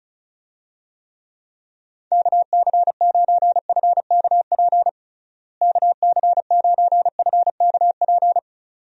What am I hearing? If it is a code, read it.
Morse code, heard as a keyed tone, about 36 wpm, 700 Hz. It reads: KC9FKP KC9FKP
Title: USA Call Signs - 6 Characters - Repeated 2x - 35wpm